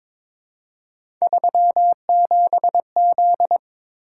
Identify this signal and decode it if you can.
Morse code, heard as a keyed tone, about 21 wpm, 700 Hz. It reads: 37Z
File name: practice / 22wpm / Three Letter-Number Combinations - 22wpm